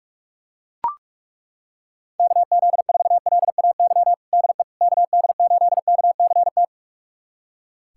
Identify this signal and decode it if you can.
Morse code, heard as a keyed tone, about 41 wpm, 690 Hz. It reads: KZ4LAY DE KD8KCT
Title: USA Callsign DE Callsign - 6 Characters - 45wpm